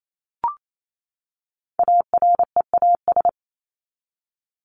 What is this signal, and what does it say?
Morse code, heard as a keyed tone, about 28 wpm, 700 Hz. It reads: AREAS
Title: ES:S Suffix - Top 100 Words with Base Words 28wpm